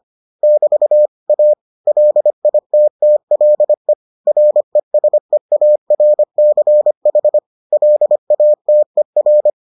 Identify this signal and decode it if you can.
Morse code, heard as a keyed tone, about 25 wpm, 600 Hz. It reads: <BT> A LITTLE RESEARCH LATER